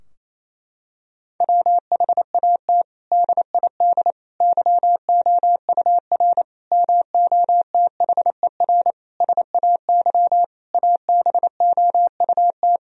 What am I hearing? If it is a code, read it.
Morse code, heard as a keyed tone, about 28 wpm, 700 Hz. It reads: WHAT DID YOUR MOTHER SAY ABOUT
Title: Sentences from Top 1000 Words - Review Part 3 - 28wpm